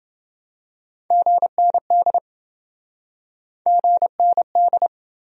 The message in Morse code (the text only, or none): GND GND